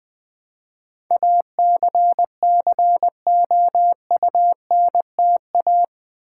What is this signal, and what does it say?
Morse code, heard as a keyed tone, about 20 wpm, 700 Hz. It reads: ACCOUNTA
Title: Build A Word - 4 Letters At A Time - 11-Letter Words - 20wpm